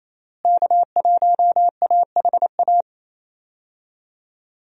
Morse code, heard as a keyed tone, about 28 wpm, 700 Hz. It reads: K1AHA
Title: USA Call Signs - 5 Characters - Repeated 3x - 28wpm